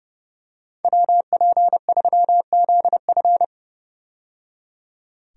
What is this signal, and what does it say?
Morse code, heard as a keyed tone, about 30 wpm, 700 Hz. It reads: WP3ZF
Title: USA Call Signs - 5 Characters - 30wpm